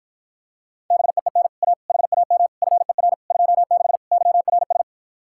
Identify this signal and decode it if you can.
Morse code, heard as a keyed tone, about 44 wpm, 700 Hz. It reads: BEEN A HAM FER 26 YRS